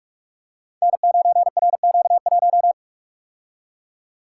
Morse code, heard as a keyed tone, about 44 wpm, 700 Hz. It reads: N9RQ1
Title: Code Groups - 5 Letter:Numbers - 45wpm